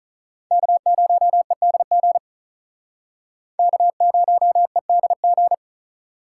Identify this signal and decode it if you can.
Morse code, heard as a keyed tone, about 45 wpm, 700 Hz. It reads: K0EDG K0EDG